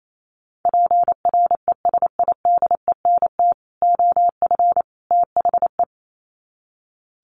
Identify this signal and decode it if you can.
Morse code, heard as a keyed tone, about 28 wpm, 700 Hz. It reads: PRESIDENT OF THE
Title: Sets of 3 Words 28wpm